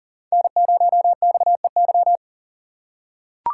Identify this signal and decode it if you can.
Morse code, heard as a keyed tone, about 42 wpm, 700 Hz. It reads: N0XEY
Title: USA Call Signs - 5 Characters - 6x Character Spacing - 40wpm